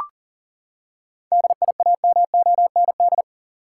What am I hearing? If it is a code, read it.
Morse code, heard as a keyed tone, about 40 wpm, 700 Hz. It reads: DIAMOND
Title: Seven Letter Words - 40wpm